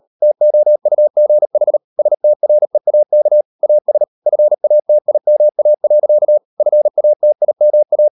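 Morse code, heard as a keyed tone, about 38 wpm, 600 Hz. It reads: TOUGH STREAK AS FATIMA. FATIMA
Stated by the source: Morse code practice at 38 WPM, sending Ken Follett's novel Never in English, from CW med Gnister.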